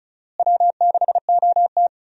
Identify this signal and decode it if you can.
Morse code, heard as a keyed tone, about 36 wpm, 700 Hz. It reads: W6OT
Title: USA Call Signs - 4 Characters - Repeated 3x - 35wpm